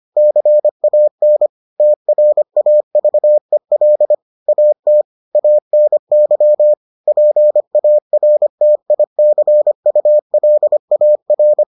Morse code, heard as a keyed tone, about 25 wpm, 600 Hz. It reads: CAN TRAVEL AT ANY PARTICULAR